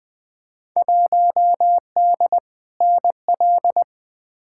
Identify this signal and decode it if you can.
Morse code, heard as a keyed tone, about 20 wpm, 700 Hz. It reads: 1D NL
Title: Field Day Exchanges Encore - 20wpm